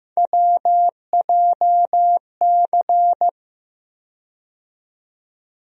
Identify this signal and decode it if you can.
Morse code, heard as a keyed tone, about 15 wpm, 700 Hz. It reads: WJC
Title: USA Call Sign Postfixes - 3 Letters 15wpm